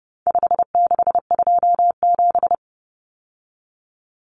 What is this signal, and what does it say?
Morse code, heard as a keyed tone, about 30 wpm, 700 Hz. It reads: 5627
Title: Common Cut Numbers - 4-Digit 30wpm